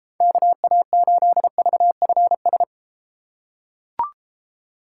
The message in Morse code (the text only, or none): KA8VFS